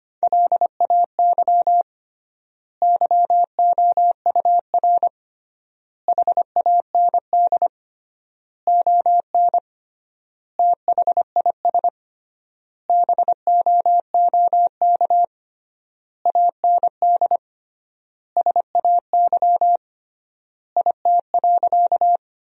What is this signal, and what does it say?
Morse code, heard as a keyed tone, about 25 wpm, 700 Hz. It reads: LAY YOUR HAND ON THIS BOOK AND SAY IT.
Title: 3x Word-Spacing - Sentences from Top 400 Words - 25wpm